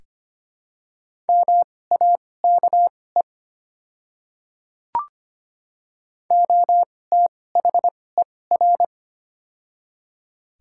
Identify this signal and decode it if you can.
Morse code, heard as a keyed tone, about 25 wpm, spaced as for 18 wpm, 690 Hz. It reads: MAKE OTHER
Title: 2x Character Spacing - Top 100 Words - 25wpm